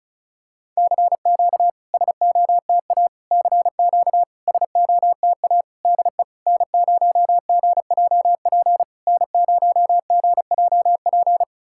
Morse code, heard as a keyed tone, about 36 wpm, 700 Hz. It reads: CQ SOTA CQ SOTA DE N0GJP N0GJP
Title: SOTA QSO Practice 35wpm